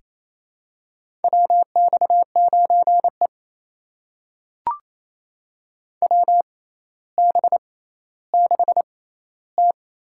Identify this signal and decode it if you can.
Morse code, heard as a keyed tone, about 28 wpm, spaced as for 14 wpm, 700 Hz. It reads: WX9E WB6T
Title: USA Call Signs - 4 Characters - 6x Character Spacing - 28wpm